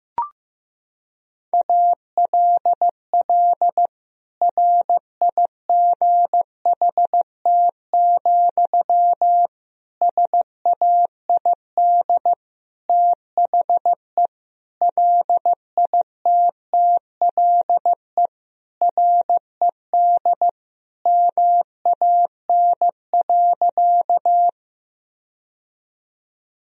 Morse code, heard as a keyed tone, about 16 wpm, 700 Hz. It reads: ALL RIGHT, SAID THE LITTLE RED MAN.
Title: Sentences from Top 1000 Words - Review Part 2 - 15wpm